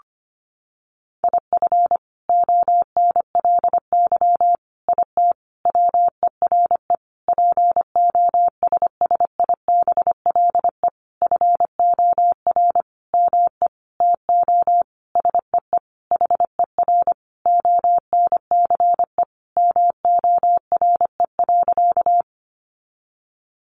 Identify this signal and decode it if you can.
Morse code, heard as a keyed tone, about 25 wpm, 700 Hz. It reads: IF ONLY IT WERE POSSIBLE FOR ME TO SEE HER ONCE MORE.
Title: Sentences from Top 500 Words 25wpm